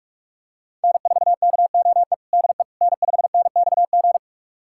Morse code, heard as a keyed tone, about 46 wpm, 700 Hz. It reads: N4KOE DE N5NXG
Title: USA Callsign DE Callsign - 5 Characters - 45wpm